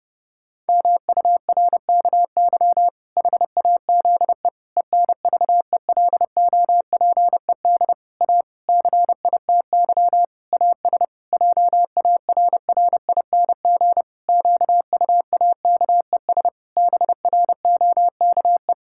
Morse code, heard as a keyed tone, about 30 wpm, 700 Hz. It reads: MURKY HAZE ENVELOPED A CITY AS JARRING QUAKES BROKE
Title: Pangram Challenge v2 - 30wpm